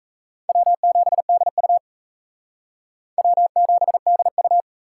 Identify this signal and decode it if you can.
Morse code, heard as a keyed tone, about 35 wpm, 700 Hz. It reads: W7DU W7DU